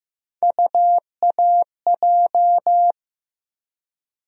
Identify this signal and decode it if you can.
Morse code, heard as a keyed tone, about 15 wpm, 700 Hz. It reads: UAJ
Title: USA Call Sign Postfixes - 3 Letters 15wpm